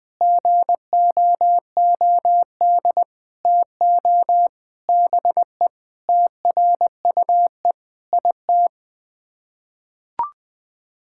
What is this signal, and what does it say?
Morse code, heard as a keyed tone, about 20 wpm, 700 Hz. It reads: GOOD TO BE TRUE IT
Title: Sets of 9 Words - 20wpm